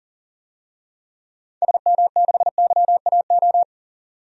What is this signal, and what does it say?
Morse code, heard as a keyed tone, about 40 wpm, 700 Hz. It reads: SM6YAO
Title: DXCC-100 Call Signs - 6 Characters - 40wpm